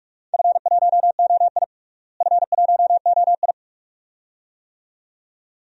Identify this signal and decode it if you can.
Morse code, heard as a keyed tone, about 42 wpm, 700 Hz. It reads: F1OI F1OI
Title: DXCC-100 Call Signs - 4 Characters - Repeated 2x - 45wpm